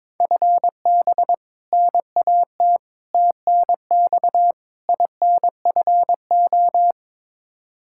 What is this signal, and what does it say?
Morse code, heard as a keyed tone, about 22 wpm, 700 Hz. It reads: FB NAT TNX INFO
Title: Practice QSO - Exchange 3 - More Info - 22wpm